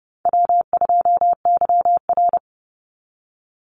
Morse code, heard as a keyed tone, about 30 wpm, 700 Hz. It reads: W2YR
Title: Call Signs - 30wpm